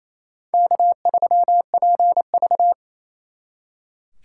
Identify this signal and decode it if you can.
Morse code, heard as a keyed tone, about 32 wpm, 700 Hz. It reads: K3PV
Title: Speed Racing - USA 1x2 Call Signs 28wpm